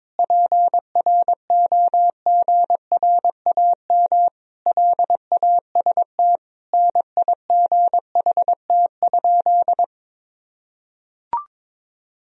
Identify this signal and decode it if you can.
Morse code, heard as a keyed tone, about 22 wpm, 700 Hz. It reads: PROGRAM LAST NIGHT?
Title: Sentences from Top 1000 Words - Review Part 4 - 22wpm